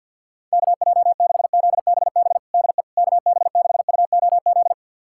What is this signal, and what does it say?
Morse code, heard as a keyed tone, about 50 wpm, 700 Hz. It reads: KJ6ZBB DE KB6UOZ
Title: USA Callsign DE Callsign 50wpm